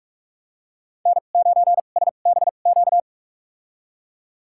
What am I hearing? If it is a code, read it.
Morse code, heard as a keyed tone, about 41 wpm, spaced as for 31 wpm, 700 Hz. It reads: N9SBQ
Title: USA Call Signs - 2x Character Spacing - 45wpm